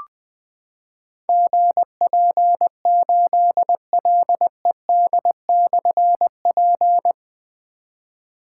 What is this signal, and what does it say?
Morse code, heard as a keyed tone, about 20 wpm, 700 Hz. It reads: GP8LED/P
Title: Difficult Call Signs 20wpm